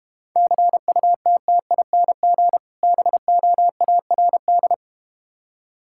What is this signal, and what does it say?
Morse code, heard as a keyed tone, about 32 wpm, 700 Hz. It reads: CUTTING BOARD